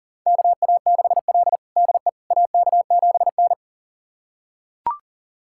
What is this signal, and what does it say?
Morse code, heard as a keyed tone, about 40 wpm, 700 Hz. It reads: KA6L DE AK7N
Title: USA Callsign DE Callsign - 4 Characters - 40wpm